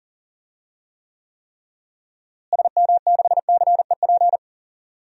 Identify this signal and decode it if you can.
Morse code, heard as a keyed tone, about 40 wpm, 700 Hz. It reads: SM6CEP
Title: DXCC-100 Call Signs - 6 Characters - Repeated 3x - 40wpm